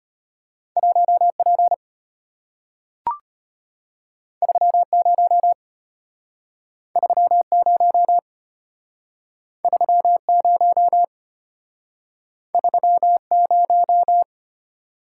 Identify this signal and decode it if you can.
Morse code, heard as a keyed tone, about 34 wpm, 700 Hz. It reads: 1P 30 30 30 30